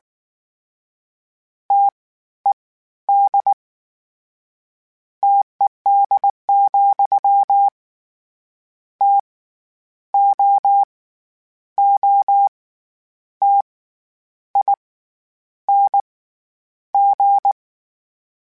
Morse code, heard as a keyed tone, about 19 wpm, 800 Hz. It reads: T E D TED, T O O T I N G
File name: Variable Spacing / Variable Spacing-mp3s / T AOEs @18